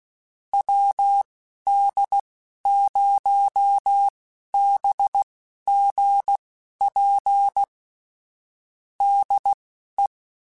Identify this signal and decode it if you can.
Morse code, heard as a keyed tone, about 16 wpm, spaced as for 11 wpm, 800 Hz. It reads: WD0BGP DE
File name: QSOs / QSOs-mp3s / WD0BGP 1@13